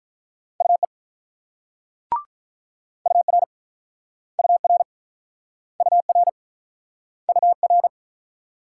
Morse code, heard as a keyed tone, about 46 wpm, 700 Hz. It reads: UE UR UR UR UR